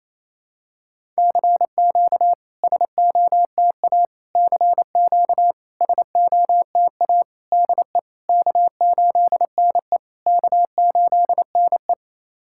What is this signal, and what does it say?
Morse code, heard as a keyed tone, about 27 wpm, 690 Hz. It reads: CQ SOTA CQ SOTA DE K8NE K8NE
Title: SOTA QSO Practice 28wpm